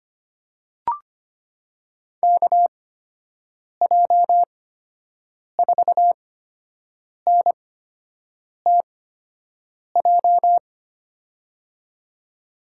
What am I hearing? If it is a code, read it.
Morse code, heard as a keyed tone, about 25 wpm, spaced as for 7 wpm, 700 Hz. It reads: KJ4NTJ
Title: USA Call Signs - 8x Character Spacing - 25wpm